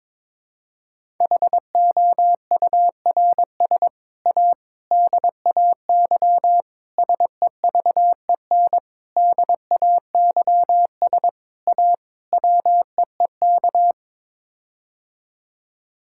Morse code, heard as a keyed tone, about 22 wpm, 700 Hz. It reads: HOURS A DAY SEVEN DAYS A WEEK
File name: practice / 22wpm / Sets of 7 Words - 22wpm